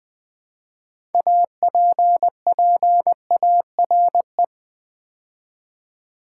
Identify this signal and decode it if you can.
Morse code, heard as a keyed tone, about 20 wpm, 700 Hz. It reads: APPARE
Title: Build A Word - 2 Letters At A Time - 8-Letter Words - 20wpm